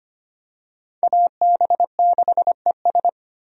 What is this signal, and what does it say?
Morse code, heard as a keyed tone, about 25 wpm, 700 Hz. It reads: AB6ES
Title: USA Call Signs - 5 Characters - Repeated 2x - 25wpm